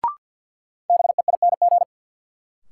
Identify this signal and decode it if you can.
Morse code, heard as a keyed tone, about 50 wpm, 700 Hz. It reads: BEING
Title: Sets of 4 Words - New Word Drill - 50wpm